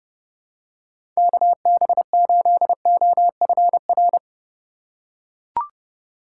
Morse code, heard as a keyed tone, about 30 wpm, 700 Hz. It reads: KB8OFR
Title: USA Call Signs - 6 Characters - 2x Character Spacing - 30wpm